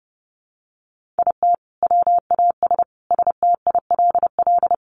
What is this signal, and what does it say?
Morse code, heard as a keyed tone, about 30 wpm, 700 Hz. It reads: IT WAS STILL
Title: Sentences from Top 1500 Words 30wpm